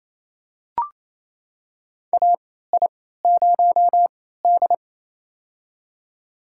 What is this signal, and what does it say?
Morse code, heard as a keyed tone, about 28 wpm, spaced as for 16 wpm, 700 Hz. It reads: AI0D
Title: USA Call Signs - 4 Characters - 3x Character Spacing - 28wpm